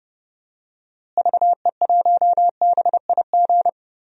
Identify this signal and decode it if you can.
Morse code, heard as a keyed tone, about 30 wpm, 700 Hz. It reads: VE1BIG
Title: DXCC-100 Call Signs - 6 Characters - Repeated 3x - 30wpm